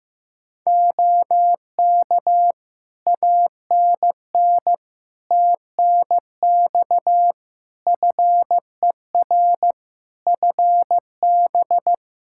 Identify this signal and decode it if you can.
Morse code, heard as a keyed tone, about 15 wpm, 700 Hz. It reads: OK ANN TNX FER FB
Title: Practice QSO - Exchange 4 - Ending - 15wpm